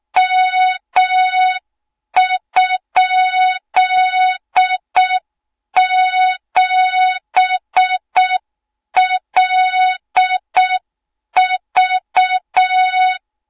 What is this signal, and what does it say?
Morse code, heard as a keyed tone, about 6 wpm, 3700 Hz. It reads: M?7LV
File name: Morze-HA2MN / hang / 20 GYAKORLAT